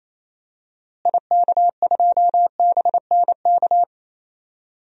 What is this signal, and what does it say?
Morse code, heard as a keyed tone, about 28 wpm, 700 Hz. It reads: IK2BNK
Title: DXCC-100 Call Signs - 6 Characters - Repeated 2x - 28wpm